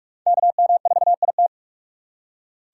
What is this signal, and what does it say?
Morse code, heard as a keyed tone, about 42 wpm, 700 Hz. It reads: KM4IT
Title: USA Call Signs - 3x Character Spacing - 45wpm